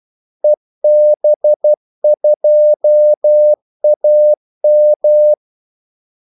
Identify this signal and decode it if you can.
Morse code, heard as a keyed tone, about 12 wpm, 600 Hz. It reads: EB2AM
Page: 2 ekstra ordmellemrum mellem kaldesignalerne